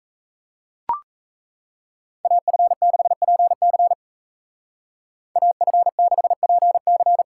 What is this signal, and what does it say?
Morse code, heard as a keyed tone, about 36 wpm, 700 Hz. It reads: AF6PC AF6PC